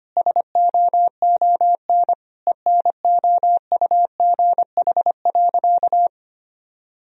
Morse code, heard as a keyed tone, about 25 wpm, 700 Hz. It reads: SOON ENOUGH.
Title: Sentences from Top 1000 Words - Review Part 3 - 25wpm